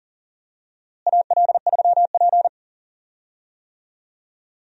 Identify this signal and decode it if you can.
Morse code, heard as a keyed tone, about 40 wpm, 700 Hz. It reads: AL3P
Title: USA Call Signs - 4 Characters - Repeated 3x - 40wpm